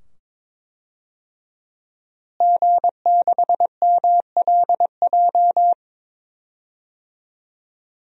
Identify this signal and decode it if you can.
Morse code, heard as a keyed tone, about 22 wpm, 700 Hz. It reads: G6MLJ
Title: DXCC-100 Call Signs - 5 Characters - Repeated 2x - 22wpm